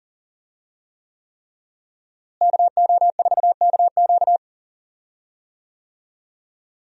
Morse code, heard as a keyed tone, about 40 wpm, 700 Hz. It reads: KO4KQ
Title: USA Call Signs - 5 Characters - Repeated 3x - 40wpm